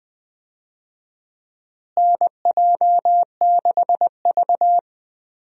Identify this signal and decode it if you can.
Morse code, heard as a keyed tone, about 20 wpm, 700 Hz. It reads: NJ6V
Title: USA Call Signs - 4 Characters - Repeated 2x - 20wpm